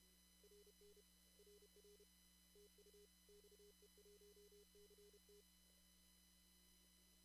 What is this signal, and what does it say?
Morse code, heard as a keyed tone, about 30 wpm, 410 Hz, with noise. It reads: RN RR TU XE1CT
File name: CW / kt5j wpx cw-op-k5pi